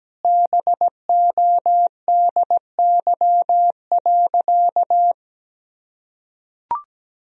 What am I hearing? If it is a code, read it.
Morse code, heard as a keyed tone, about 17 wpm, 700 Hz. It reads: BODY.
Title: Sentences from Top 1000 Words - Review Part 2 - 17wpm